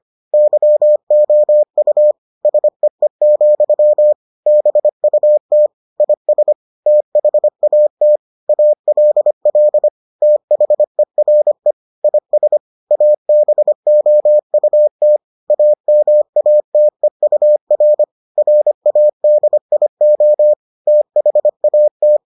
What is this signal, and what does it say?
Morse code, heard as a keyed tone, about 25 wpm, 600 Hz. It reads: YOU SEE, BUT IS THAT ALL THERE IS ABOUT AMATEUR RADIO THAT